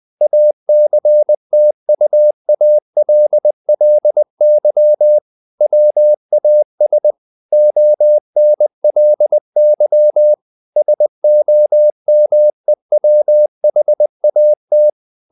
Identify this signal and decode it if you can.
Morse code, heard as a keyed tone, about 20 wpm, 600 Hz. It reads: ACTUALLY WAS ONLY SOMEWHAT
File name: the old man and the sea / the old man and the sea part 1 20wpm